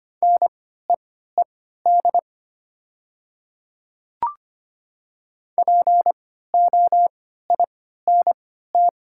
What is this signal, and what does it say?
Morse code, heard as a keyed tone, about 25 wpm, spaced as for 14 wpm, 700 Hz. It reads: NEED POINT
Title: 3x Character Spacing - Top 200 Words - 25wpm